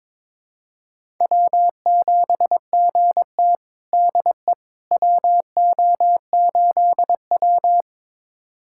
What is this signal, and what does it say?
Morse code, heard as a keyed tone, about 22 wpm, 700 Hz. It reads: W7GT DE WO8W
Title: USA Callsign DE Callsign - 4 Characters - 22wpm